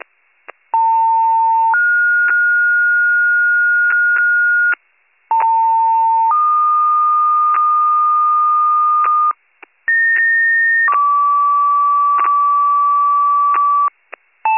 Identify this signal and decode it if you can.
Two-tone page, heard as a one-second tone, then a three-second tone, about 900 then 1430 Hz, about 900 then 1220 Hz, about 1850 then 1150 Hz. It is heard from 0.7 to 13.9 s.